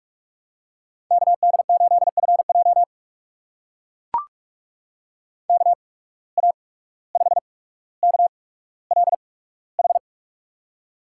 Morse code, heard as a keyed tone, about 44 wpm, spaced as for 18 wpm, 700 Hz. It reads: KD8FJ KA5KLH
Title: USA Call Signs - 8x Character Spacing - 45wpm